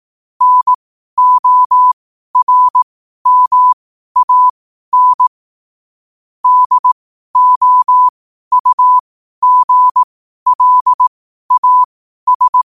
Morse code, heard as a keyed tone, about 18 wpm, spaced as for 12 wpm, 1000 Hz. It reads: NORMAN DOUGLAS
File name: qotdfiles / 12wpm